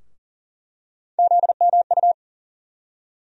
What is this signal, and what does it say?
Morse code, heard as a keyed tone, about 40 wpm, 700 Hz. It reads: ZMU
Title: USA Call Sign Postfixes - 3 Letters 40wpm